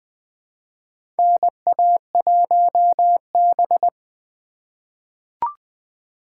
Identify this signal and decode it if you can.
Morse code, heard as a keyed tone, about 21 wpm, 700 Hz. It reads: NA1B
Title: USA Call Signs - 4 Characters - 4x Character Spacing - 20wpm